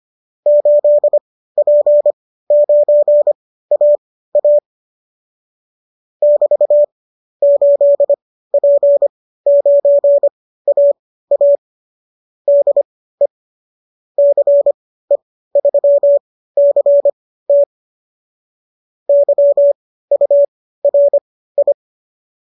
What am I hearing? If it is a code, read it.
Morse code, heard as a keyed tone, about 25 wpm, spaced as for 15 wpm, 590 Hz. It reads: 8P9AA <BT> 8P9AA DE CE3CT YURI